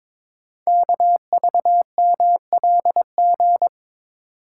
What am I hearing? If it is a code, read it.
Morse code, heard as a keyed tone, about 22 wpm, 700 Hz. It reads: KVMLG
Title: Code Groups - 5 Letters - 22wpm